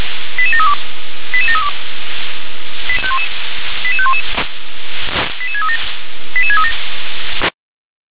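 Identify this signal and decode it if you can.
Five-tone selective call (ZVEI), heard as quick runs of five tones, tones of about 70 ms. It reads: 88732 88732 88511 88637 88637
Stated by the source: ZVEI five-tone selective call 88622 received on an SDR.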